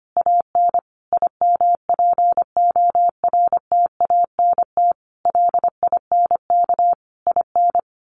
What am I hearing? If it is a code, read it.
Morse code, heard as a keyed tone, about 25 wpm, 700 Hz. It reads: AN IMPORTANT LINK IN
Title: A Sound Of Thunder 25wpm